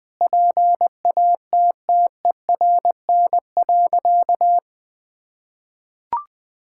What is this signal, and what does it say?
Morse code, heard as a keyed tone, about 21 wpm, 700 Hz. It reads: PATTERN.
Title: Sentences with 5-Word Phrases - 20wpm